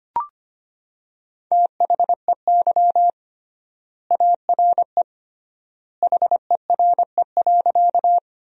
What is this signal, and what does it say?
Morse code, heard as a keyed tone, about 25 wpm, 700 Hz. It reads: THEY ARE HERE.
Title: 3x Word-Spacing - Sentences from Top 200 Words - 25wpm